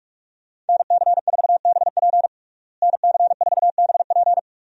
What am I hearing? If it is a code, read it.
Morse code, heard as a keyed tone, about 47 wpm, 700 Hz. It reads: NC4BP NC4BP
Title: USA Call Signs - 5 Characters - Repeated 2x - 45wpm